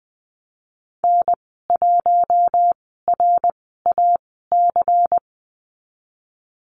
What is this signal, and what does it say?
Morse code, heard as a keyed tone, about 20 wpm, spaced as for 15 wpm, 700 Hz. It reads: N1RAC
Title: 161 Lesson 26g - Letter g - Call Signs 20wpm